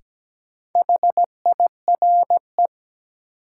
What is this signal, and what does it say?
Morse code, heard as a keyed tone, about 17 wpm, 700 Hz. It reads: HIRE
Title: Sentences from Top 1400 Words - New Word Drill - 17wpm